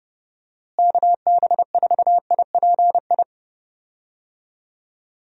Morse code, heard as a keyed tone, about 31 wpm, 700 Hz. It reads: KB4IPI
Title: USA Call Signs - 6 Characters - Repeated 2x - 30wpm